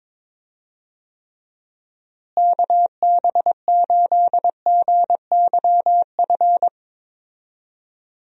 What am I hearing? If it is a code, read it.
Morse code, heard as a keyed tone, about 22 wpm, 700 Hz. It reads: KB8GYF
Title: USA Call Signs - 6 Characters - Repeated 3x - 22wpm